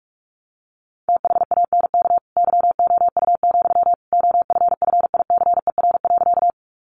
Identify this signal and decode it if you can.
Morse code, heard as a keyed tone, about 44 wpm, 700 Hz. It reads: THANK YOU, OFFICER.
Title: Sentences from Top 1200 Words 45wpm